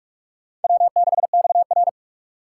Morse code, heard as a keyed tone, about 46 wpm, 700 Hz. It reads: W6XR
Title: K1USN SST QSO Practice - 45wpm